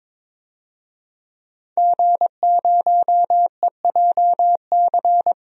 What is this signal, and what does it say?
Morse code, heard as a keyed tone, about 22 wpm, 700 Hz. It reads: G0EJC
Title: DXCC-100 Call Signs - 5 Characters - Repeated 3x - 22wpm